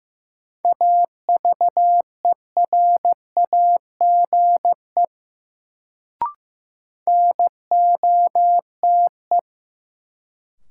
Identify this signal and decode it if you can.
Morse code, heard as a keyed tone, about 15 wpm, 700 Hz. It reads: AVERAGE NOTE
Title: Sets of 5 Words - New Word Drill - 15wpm